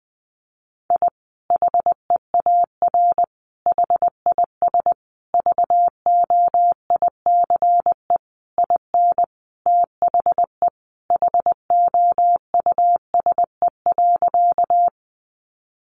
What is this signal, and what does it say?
Morse code, heard as a keyed tone, about 20 wpm, 700 Hz. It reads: I HEAR HIS VOICE IN THE HOUSE.
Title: Sentences from Top 500 Words 20wpm